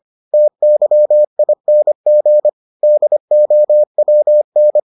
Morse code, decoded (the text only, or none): TYING DOWN